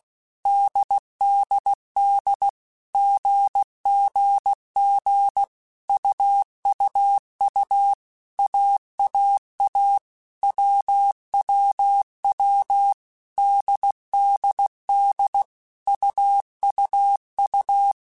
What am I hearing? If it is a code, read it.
Morse code, heard as a keyed tone, about 16 wpm, 800 Hz. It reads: DDD GGG UUU AAA WWW DDD UUU
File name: Basic Letter Groups By7Groups / AWUNDG / AWUNDG@15